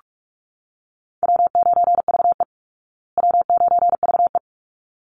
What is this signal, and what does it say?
Morse code, heard as a keyed tone, about 44 wpm, 700 Hz. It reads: W9VE W9VE
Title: Easy Call Signs Repeated 2x - 45wpm